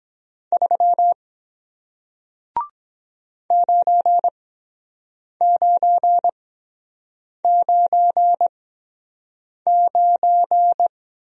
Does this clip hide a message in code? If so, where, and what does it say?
Yes: Morse code from 0.5 to 10.9 s, heard as a keyed tone, about 22 wpm, spaced as for 7 wpm, 700 Hz. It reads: 3 9999